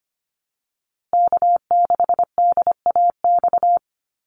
Morse code, heard as a keyed tone, about 25 wpm, 700 Hz. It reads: K6DAX
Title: USA Call Signs Repeated 3x 25wpm